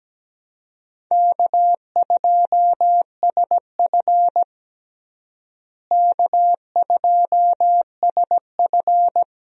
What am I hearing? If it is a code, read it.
Morse code, heard as a keyed tone, about 17 wpm, 700 Hz. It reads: K2SF K2SF
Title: USA Call Signs - 4 Characters - Repeated 3x - 17wpm